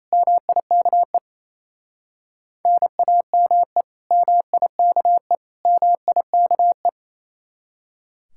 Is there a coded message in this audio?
Yes: Morse code from 0.1 to 6.9 s, heard as a keyed tone, about 35 wpm, 700 Hz. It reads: MIKE NAME MIKE MIKE